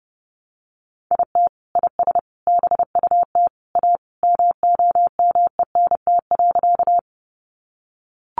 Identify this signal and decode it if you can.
Morse code, heard as a keyed tone, about 30 wpm, 700 Hz. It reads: IT IS BUT A MOMENT.
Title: Sentences from Top 1000 Words 30wpm